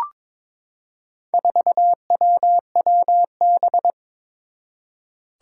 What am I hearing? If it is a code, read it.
Morse code, heard as a keyed tone, about 22 wpm, 700 Hz. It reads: 4WWB
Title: Four Challenging Letter-Number Combinations 22wpm